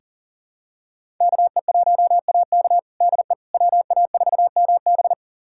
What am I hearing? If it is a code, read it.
Morse code, heard as a keyed tone, about 37 wpm, 700 Hz. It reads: KE1AK DE WA4MB
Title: USA Callsign DE Callsign - 5 Characters - 40wpm